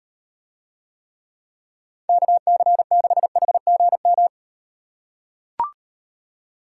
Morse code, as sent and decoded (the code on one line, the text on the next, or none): -.- -.-. -.... .... --. --
KC6HGM